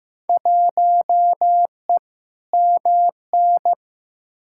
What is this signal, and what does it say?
Morse code, heard as a keyed tone, about 15 wpm, 700 Hz. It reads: 1E MN
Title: Field Day Exchanges - 15wpm